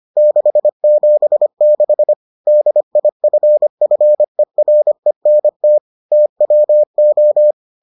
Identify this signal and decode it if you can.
Morse code, heard as a keyed tone, about 25 wpm, 600 Hz. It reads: 676 DIFFERENT TWO